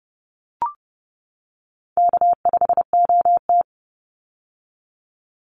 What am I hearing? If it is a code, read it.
Morse code, heard as a keyed tone, about 30 wpm, 700 Hz. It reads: K5OT
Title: Easy Call Signs - 30wpm